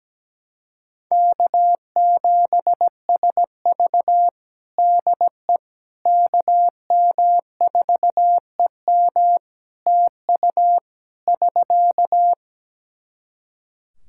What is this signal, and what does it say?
Morse code, heard as a keyed tone, about 17 wpm, 700 Hz. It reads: K7SV DE KM4EM TU <SK>
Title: Practice QSO - Exchange 4 - Ending - 17wpm